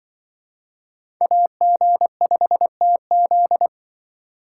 Morse code, heard as a keyed tone, about 23 wpm, 700 Hz. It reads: AG5TZ
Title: Speed Racing - USA 2x2 Call Signs 20wpm